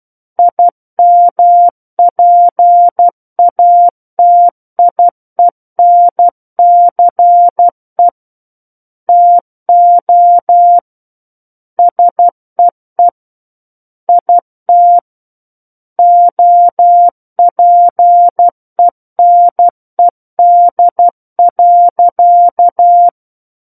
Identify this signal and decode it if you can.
Morse code, heard as a keyed tone, about 12 wpm, 700 Hz. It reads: IMPATIENCE TO SEE IT OPENED.
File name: Book One / 03-Chapter 3 - 12 WPM